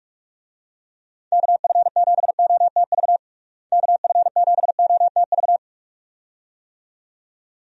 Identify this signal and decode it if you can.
Morse code, heard as a keyed tone, about 46 wpm, 700 Hz. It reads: KF7OTV KF7OTV
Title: USA Call Signs - 6 Characters - Repeated 2x - 45wpm